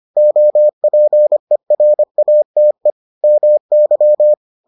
Morse code, heard as a keyed tone, about 25 wpm, 600 Hz. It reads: OPERATE MY